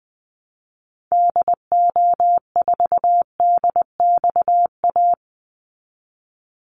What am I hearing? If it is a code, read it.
Morse code, heard as a keyed tone, about 20 wpm, 700 Hz. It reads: DO4DXA
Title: Call Signs Repeated 3x - 20wpm